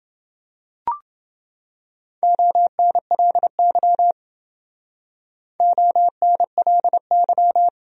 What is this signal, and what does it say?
Morse code, heard as a keyed tone, about 28 wpm, 700 Hz. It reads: ONLY ONLY